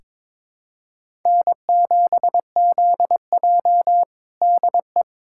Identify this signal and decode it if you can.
Morse code, heard as a keyed tone, about 22 wpm, 700 Hz. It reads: N7ZJ DE
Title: USA Callsign DE Callsign - 4 Characters - 22wpm